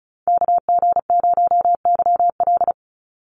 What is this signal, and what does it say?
Morse code, heard as a keyed tone, about 34 wpm, 700 Hz. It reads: KG0YL
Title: Call Signs Repeated 2x - 35wpm